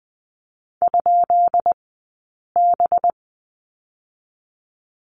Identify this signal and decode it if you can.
Morse code, heard as a keyed tone, about 20 wpm, 700 Hz. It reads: ? B
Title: 190 Lesson 31d - Letter v - Two Characters 20wpm